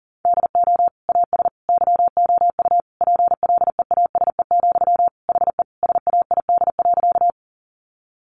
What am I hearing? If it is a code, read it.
Morse code, heard as a keyed tone, about 40 wpm, 700 Hz. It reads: DO AS YOU PLEASE, HE SAID.
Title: Sentences from Top 1500 Words 40wpm